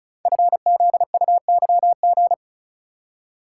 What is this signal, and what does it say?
Morse code, heard as a keyed tone, about 36 wpm, 700 Hz. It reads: FZUYG
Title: Code Groups - 5 Letter:Numbers - 35wpm